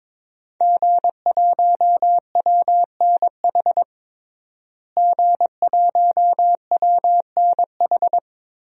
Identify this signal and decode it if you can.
Morse code, heard as a keyed tone, about 22 wpm, 700 Hz. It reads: G1WNH G1WNH
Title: DXCC-100 Call Signs - 5 Characters - Repeated 3x - 22wpm